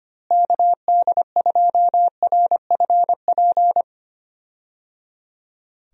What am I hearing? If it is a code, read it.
Morse code, heard as a keyed tone, about 25 wpm, 700 Hz. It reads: KD2RFP
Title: USA Call Signs - 6 Characters - Repeated 3x - 25wpm